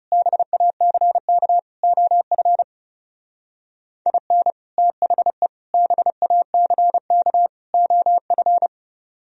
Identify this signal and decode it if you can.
Morse code, heard as a keyed tone, about 38 wpm, 700 Hz. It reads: BACK OF IN THE BACK OF